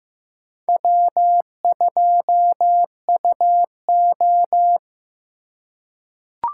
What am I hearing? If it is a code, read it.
Morse code, heard as a keyed tone, about 15 wpm, 700 Hz. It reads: W2UO
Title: USA Call Signs - 2x Character Spacing - 15wpm